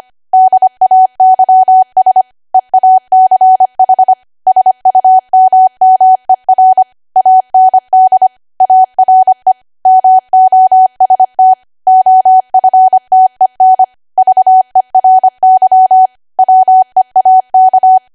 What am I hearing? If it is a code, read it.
Morse code, heard as a keyed tone, about 25 wpm, 750 Hz. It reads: DAYS EACH SUMMER AND ARE MOST OFTEN VERY WEAK